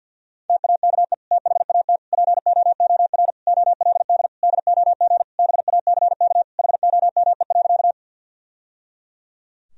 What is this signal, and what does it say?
Morse code, heard as a keyed tone, about 50 wpm, 700 Hz. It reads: TAKE THAT POOR OLD DOG BACK HOME.
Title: Sentences from Top 1000 Words - Review Part 2 - 50wpm